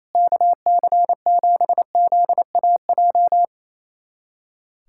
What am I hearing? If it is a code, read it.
Morse code, heard as a keyed tone, about 28 wpm, 700 Hz. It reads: KC7ZAJ
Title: USA Callsign DE Callsign - 6 Characters - 28wpm